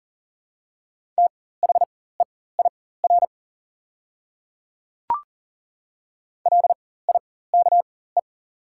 Morse code, heard as a keyed tone, about 40 wpm, spaced as for 19 wpm, 700 Hz. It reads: THEIR LIKE
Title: 4x Character Spacing - Top 100 Words - 40wpm